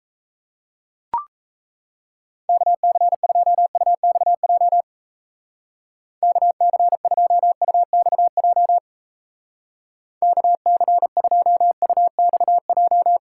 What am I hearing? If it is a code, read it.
Morse code, heard as a keyed tone, about 35 wpm, 700 Hz. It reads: KC2UXJ KC2UXJ KC2UXJ
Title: Speed Racing - USA Call Signs 28wpm